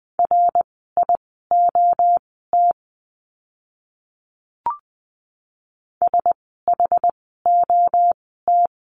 Morse code, heard as a keyed tone, about 20 wpm, spaced as for 15 wpm, 700 Hz. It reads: RIOT SHOT
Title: 061 Lesson 11f - Letter h - Words 20wpm